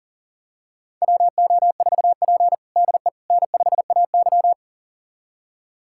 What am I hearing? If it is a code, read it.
Morse code, heard as a keyed tone, about 40 wpm, 700 Hz. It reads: WO4P DE N5AY
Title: USA Callsign DE Callsign - 4 Characters - 40wpm